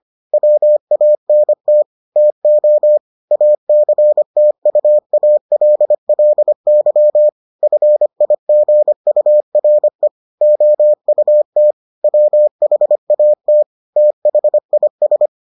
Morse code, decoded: WANT TO ACTUALLY FIGURE OUT WHAT THIS

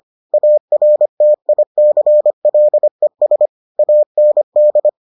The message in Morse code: ARTICLES AND

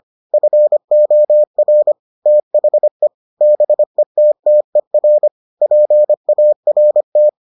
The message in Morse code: FOR THE BETTER PART

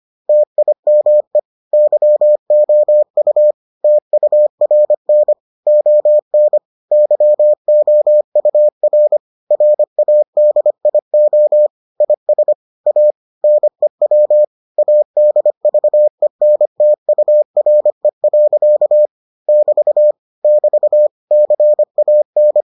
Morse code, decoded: TIME YOU TURN ON YOUR RADIO IS A NEW ADVENTURE. <BT> <BT> CAN

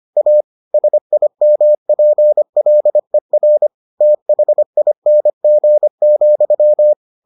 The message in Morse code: A SIMPLER THING,

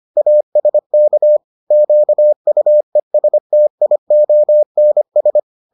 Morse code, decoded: ASK QUESTIONS

